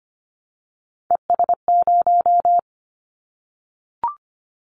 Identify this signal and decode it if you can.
Morse code, heard as a keyed tone, about 25 wpm, 700 Hz. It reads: ES0
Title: Callsign Prefixes for Top 100 DXCC - 25wpm